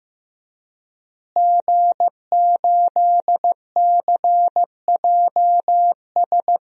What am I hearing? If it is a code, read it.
Morse code, heard as a keyed tone, about 15 wpm, 700 Hz. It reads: G8CJS
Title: DXCC-100 Call Signs - 5 Characters - Repeated 3x - 15wpm